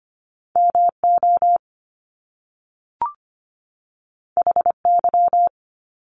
Morse code, heard as a keyed tone, about 25 wpm, 700 Hz. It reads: MO HY
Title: Top Two Letter Combinations in Words 25wpm